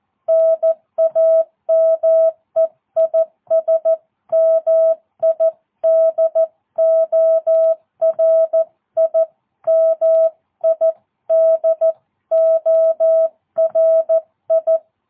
Morse code, heard as a keyed tone, about 14 wpm, 660 Hz. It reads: NAME IS MIDORI MIDORI
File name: jh8cbh / 2022cw